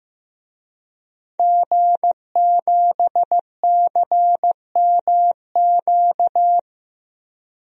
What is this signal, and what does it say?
Morse code, heard as a keyed tone, about 15 wpm, 700 Hz. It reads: G7CMQ
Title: DXCC-100 Call Signs - 5 Characters - Repeated 2x - 15wpm